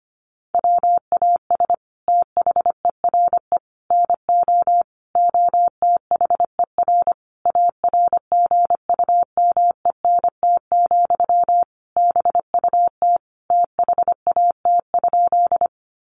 Morse code, heard as a keyed tone, about 25 wpm, 700 Hz. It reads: WAS THERE NO OTHER ARGUMENT, BUT THAT?
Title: Sentences from Top 1400 Words 25wpm